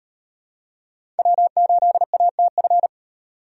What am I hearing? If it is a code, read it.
Morse code, heard as a keyed tone, about 33 wpm, 700 Hz. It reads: W8ATF